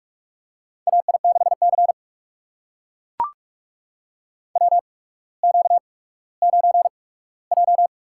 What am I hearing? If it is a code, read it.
Morse code, heard as a keyed tone, about 44 wpm, spaced as for 19 wpm, 700 Hz. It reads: AI6C WQ9J
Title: USA Call Signs - 4 Characters - 8x Character Spacing - 45wpm